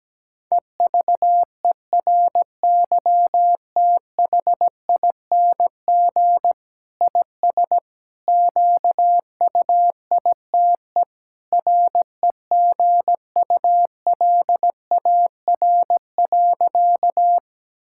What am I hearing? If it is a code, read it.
Morse code, heard as a keyed tone, about 17 wpm, 700 Hz. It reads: EVERYTHING IS QUITE REGULAR.
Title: Sentences from Top 1000 Words - Review Part 6 - 17wpm